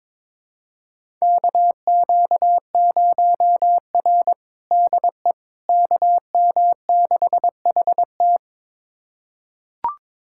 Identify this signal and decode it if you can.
Morse code, heard as a keyed tone, about 22 wpm, 700 Hz. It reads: KQ0R DE KM6HT
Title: USA Callsign DE Callsign 22wpm